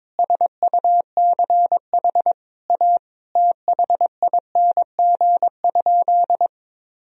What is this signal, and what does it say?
Morse code, heard as a keyed tone, about 22 wpm, 700 Hz. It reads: SUCH A THING?
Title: Sentences from Top 200 Words and Abbreviations 22wpm